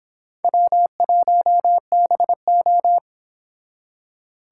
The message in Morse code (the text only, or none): W1BO